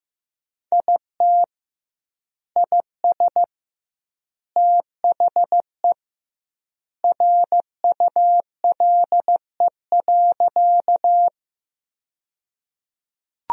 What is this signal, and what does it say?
Morse code, heard as a keyed tone, about 15 wpm, 700 Hz. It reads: IT IS THE RULE.
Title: 2x Word-Spacing - Sentences from Top 400 Words - 15wpm